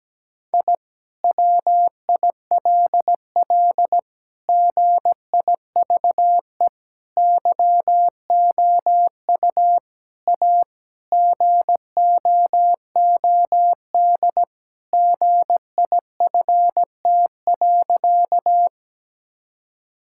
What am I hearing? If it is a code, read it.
Morse code, heard as a keyed tone, about 17 wpm, 700 Hz. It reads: I WILL GIVE YOU A GOOD GIFT.
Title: Sentences from Top 1000 Words - Review Part 6 - 17wpm